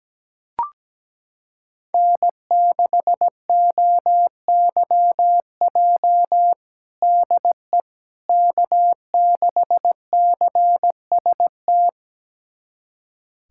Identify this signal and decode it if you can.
Morse code, heard as a keyed tone, about 17 wpm, 700 Hz. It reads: N6OYJ DE K6CST
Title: USA Callsign DE Callsign - 5 Characters - 17wpm